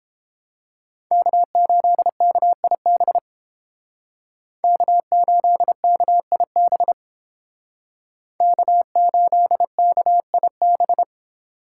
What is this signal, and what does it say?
Morse code, heard as a keyed tone, about 31 wpm, 700 Hz. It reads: K8KIB K8KIB K8KIB